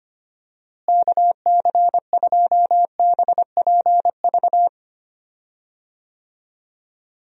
Morse code, heard as a keyed tone, about 25 wpm, 700 Hz. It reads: KC2BPV
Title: USA Call Signs - 6 Characters - Repeated 2x - 25wpm